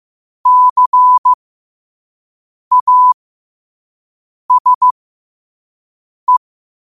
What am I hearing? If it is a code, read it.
Morse code, heard as a keyed tone, about 15 wpm, spaced as for 5 wpm, 1000 Hz. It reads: CASE
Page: 5 WPM morse code quotes for Wed, 30 Jul 2025.